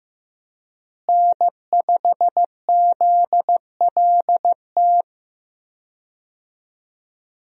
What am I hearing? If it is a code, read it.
Morse code, heard as a keyed tone, about 15 wpm, 700 Hz. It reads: N5ZLT
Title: USA Call Signs - 5 Characters - Repeated 2x - 15wpm